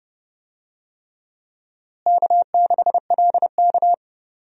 Morse code, heard as a keyed tone, about 30 wpm, 700 Hz. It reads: K6LK
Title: USA Call Signs - 4 Characters - Repeated 3x - 30wpm